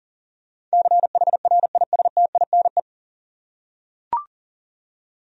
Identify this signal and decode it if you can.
Morse code, heard as a keyed tone, about 40 wpm, 700 Hz. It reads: CHRISTINE
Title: USA Common Names - 40wpm